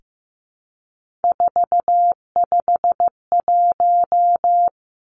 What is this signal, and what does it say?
Morse code, heard as a keyed tone, about 15 wpm, 700 Hz. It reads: 451
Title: Three Challenging Letter-Number Combinations 15wpm